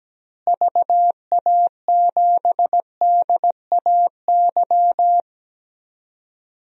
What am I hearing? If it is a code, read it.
Morse code, heard as a keyed tone, about 17 wpm, 700 Hz. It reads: VA7DAY
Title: DXCC-100 Call Signs - 6 Characters - Repeated 3x - 17wpm